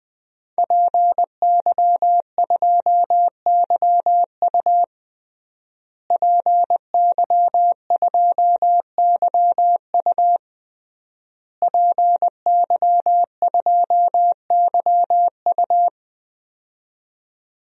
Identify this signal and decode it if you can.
Morse code, heard as a keyed tone, about 21 wpm, 700 Hz. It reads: PY2YU PY2YU PY2YU
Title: DXCC-100 Call Signs - 5 Characters - Repeated 3x - 20wpm